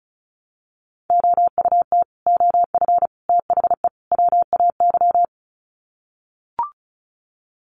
Morse code, heard as a keyed tone, about 34 wpm, 700 Hz. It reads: OUT OF THE WAY
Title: Sets of 4 Words 35wpm